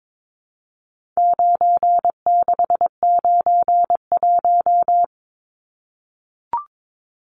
Morse code, heard as a keyed tone, about 22 wpm, 700 Hz. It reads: 9691
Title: Numbers - 4-Digit 22wpm